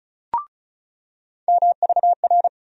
Morse code, heard as a keyed tone, about 36 wpm, 700 Hz. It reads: MVR
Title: USA Call Sign Postfixes - 3 Letters 35wpm